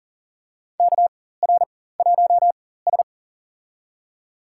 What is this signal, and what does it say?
Morse code, heard as a keyed tone, about 40 wpm, spaced as for 19 wpm, 700 Hz. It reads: KR1S
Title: USA Call Signs - 4x Character Spacing - 40wpm